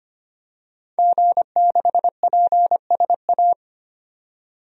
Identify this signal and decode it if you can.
Morse code, heard as a keyed tone, about 25 wpm, 700 Hz. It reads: G6PSA
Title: DXCC-100 Call Signs - 5 Characters - 25wpm